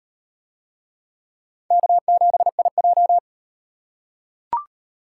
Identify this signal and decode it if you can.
Morse code, heard as a keyed tone, about 33 wpm, 700 Hz. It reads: K7IJ